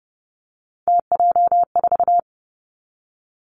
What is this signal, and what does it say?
Morse code, heard as a keyed tone, about 32 wpm, 700 Hz. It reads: TJ4
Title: Three Challenging Letter-Number Combinations 30wpm